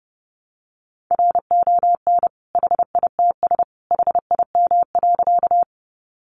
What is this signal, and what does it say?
Morse code, heard as a keyed tone, about 30 wpm, 700 Hz. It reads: RON HITS HIM.
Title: Additional Easy Phrases 30wpm